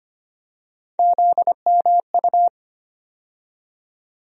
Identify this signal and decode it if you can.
Morse code, heard as a keyed tone, about 25 wpm, 700 Hz. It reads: ZMU
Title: USA Call Sign Postfixes - 3 Letters 25wpm